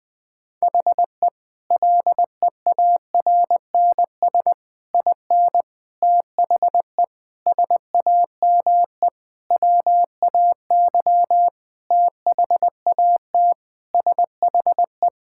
Morse code, heard as a keyed tone, about 20 wpm, 700 Hz. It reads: HE LEARNS IN THE SAME WAY THAT SHE
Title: Sentences with 5-Word Phrases - 20wpm